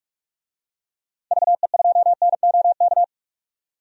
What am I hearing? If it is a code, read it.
Morse code, heard as a keyed tone, about 46 wpm, 700 Hz. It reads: VE2NOK
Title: DXCC-100 Call Signs - 6 Characters - 45wpm